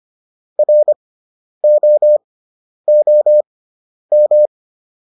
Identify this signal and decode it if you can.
Morse code, heard as a keyed tone, about 25 wpm, spaced as for 10 wpm, 600 Hz. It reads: ROOM